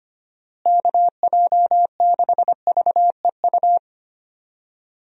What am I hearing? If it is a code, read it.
Morse code, heard as a keyed tone, about 25 wpm, 700 Hz. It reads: KJ6VEU
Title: USA Call Signs - 6 Characters - Repeated 3x - 25wpm